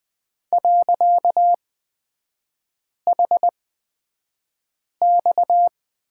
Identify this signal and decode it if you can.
Morse code, heard as a keyed tone, about 20 wpm, spaced as for 5 wpm, 700 Hz. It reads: .HX
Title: 255 Lesson 41e - Period - Three Characters 20wpm